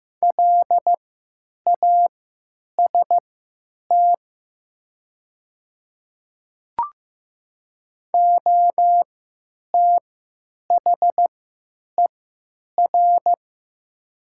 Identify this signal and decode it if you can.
Morse code, heard as a keyed tone, about 15 wpm, spaced as for 9 wpm, 700 Hz. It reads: LAST OTHER
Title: 3x Character Spacing - Top 300 Words - 15wpm